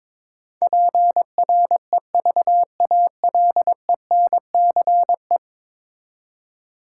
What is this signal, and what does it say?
Morse code, heard as a keyed tone, about 22 wpm, 700 Hz. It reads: PREVALENCE
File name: practice / 22wpm / 10 Letter Words - First 3 Letters Given - 22wpm